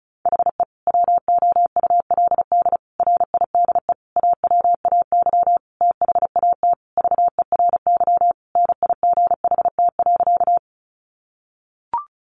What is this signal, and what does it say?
Morse code, heard as a keyed tone, about 34 wpm, 700 Hz. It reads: HE WOULD RIDE AWAY THAT VERY NIGHT.
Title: Sentences from Top 1200 Words 35wpm